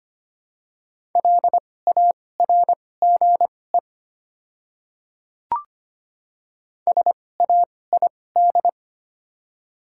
Morse code, heard as a keyed tone, about 25 wpm, spaced as for 18 wpm, 700 Hz. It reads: LARGE SAID
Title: 2x Character Spacing - Top 200 Words - 25wpm